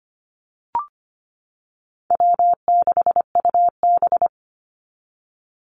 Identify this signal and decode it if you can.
Morse code, heard as a keyed tone, about 25 wpm, 700 Hz. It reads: W6UB
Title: NA SOTA Activator Callsigns 25wpm